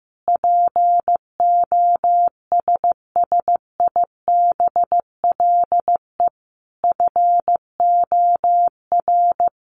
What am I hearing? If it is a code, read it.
Morse code, heard as a keyed tone, about 14 wpm, 700 Hz. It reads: POSSIBLE FOR
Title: Sentences from Top 900 Words 15wpm